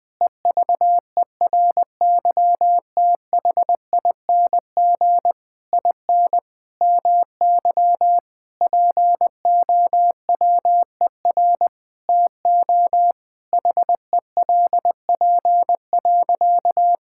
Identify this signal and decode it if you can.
Morse code, heard as a keyed tone, about 21 wpm, 700 Hz. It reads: EVERYTHING IN MY POWER TO HELP.
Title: Sentences with 8-Word Phrases - 20wpm